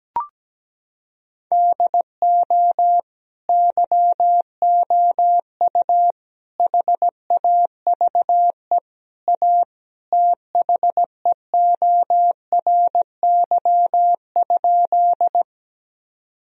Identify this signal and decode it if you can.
Morse code, heard as a keyed tone, about 17 wpm, 700 Hz. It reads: DO YOU HAVE A THEORY?
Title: Sentences from Top 1000 Words - Review Part 3 - 17wpm